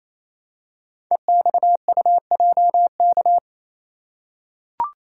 Morse code, heard as a keyed tone, about 27 wpm, 700 Hz. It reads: EXUJK
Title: Code Groups - 5 Letters - 28wpm